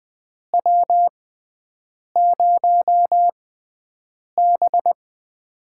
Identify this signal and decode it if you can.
Morse code, heard as a keyed tone, about 20 wpm, spaced as for 7 wpm, 700 Hz. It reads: W0B
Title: USA Call Signs - 4 Characters - 6x Character Spacing - 20wpm